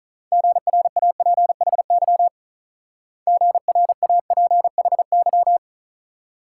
Morse code, heard as a keyed tone, about 40 wpm, 700 Hz. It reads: GRAPHY GRAPHY
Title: Speed-Racing Multi-letter Phonemes - 35wpm